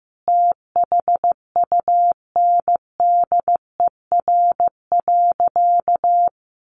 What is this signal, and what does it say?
Morse code, heard as a keyed tone, about 15 wpm, 700 Hz. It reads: THUNDER.
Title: A Sound Of Thunder 15wpm